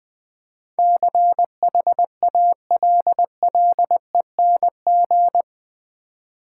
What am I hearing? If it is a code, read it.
Morse code, heard as a keyed tone, about 20 wpm, 700 Hz. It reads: CHALLENG
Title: Build A Word - 4 Letters At A Time - 11-Letter Words - 20wpm